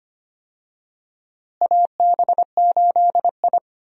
Morse code, heard as a keyed tone, about 25 wpm, 700 Hz. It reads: AB8I
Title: USA Call Signs - 4 Characters - Repeated 2x - 25wpm